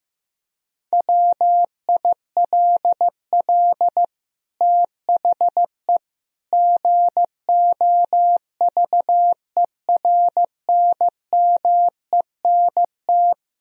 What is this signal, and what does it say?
Morse code, heard as a keyed tone, about 15 wpm, 700 Hz. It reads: WILL THE GOVERNMENT
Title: Sentences from Top 1000 Words - Review Part 3 - 15wpm